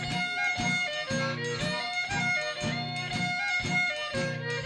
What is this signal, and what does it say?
Emergency Alert System (SAME) tones, heard as harsh buzzing bursts. It is heard from 0.2 to 4.0 s.